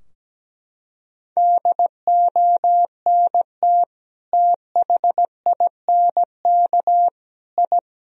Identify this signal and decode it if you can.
Morse code, heard as a keyed tone, about 17 wpm, 700 Hz. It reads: DONT THINK I
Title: Sentences from Top 1000 Words - Review Part 3 - 17wpm